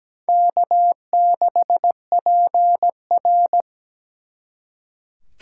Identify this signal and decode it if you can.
Morse code, heard as a keyed tone, about 20 wpm, 700 Hz. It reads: K6PR
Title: Speed Racing - USA 1x2 Call Signs 17wpm